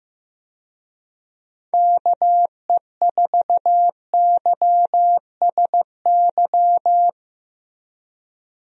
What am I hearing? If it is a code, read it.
Morse code, heard as a keyed tone, about 15 wpm, 700 Hz. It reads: KE4YSY
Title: USA Call Signs - 6 Characters - Repeated 2x - 15wpm